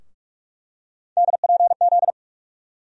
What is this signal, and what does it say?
Morse code, heard as a keyed tone, about 42 wpm, 700 Hz. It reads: DPZ
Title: USA Call Sign Postfixes - 3 Letters 45wpm